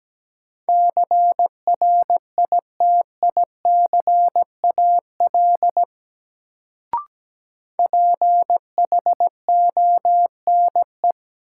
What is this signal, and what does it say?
Morse code, heard as a keyed tone, about 17 wpm, 700 Hz. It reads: CRITICAL PHONE
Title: Sentences from Top 1100 Words - New Word Drill - 17wpm